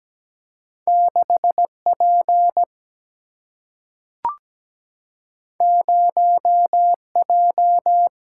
Morse code, heard as a keyed tone, about 17 wpm, 700 Hz. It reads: 6P 0J
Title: Single-Letter Single-Number 17wpm